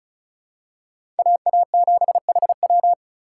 Morse code, heard as a keyed tone, about 34 wpm, 700 Hz. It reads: AA7HW
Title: USA Call Signs - 5 Characters - 3x Character Spacing - 35wpm